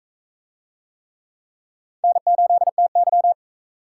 Morse code, heard as a keyed tone, about 39 wpm, 700 Hz. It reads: N8TY